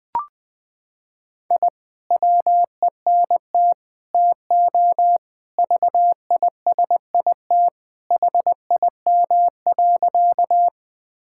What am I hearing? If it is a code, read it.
Morse code, heard as a keyed tone, about 20 wpm, 700 Hz. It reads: I WENT TO VISIT HIM.
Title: Sentences from Top 1000 Words - Review Part 6 - 20wpm